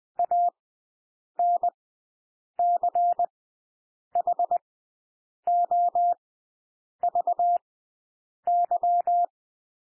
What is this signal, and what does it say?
Morse code, heard as a keyed tone, about 20 wpm, spaced as for 8 wpm, 700 Hz. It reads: ANCHOVY
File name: cwmp3 / W1 SOUND WORDS 20 wpm